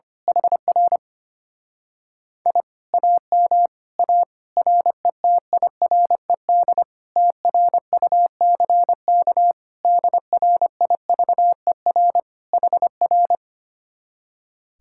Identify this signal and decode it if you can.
Morse code, heard as a keyed tone, about 25 wpm, 700 Hz. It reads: HR I AM A RETIRED TRUCK DRIVER HR
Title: Speed-Racing Common QSO Phrases Encore - 25wpm